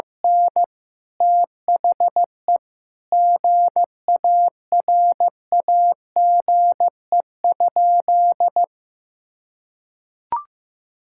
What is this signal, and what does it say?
Morse code, heard as a keyed tone, about 16 wpm, 700 Hz. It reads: N THE GARAGE?
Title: Sentences from Top 1600 Words 15wpm